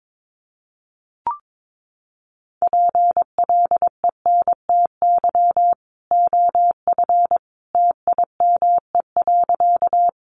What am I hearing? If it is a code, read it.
Morse code, heard as a keyed tone, about 22 wpm, 700 Hz. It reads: PLENTY OF TIME.
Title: Sentences from Top 1300 Words 22wpm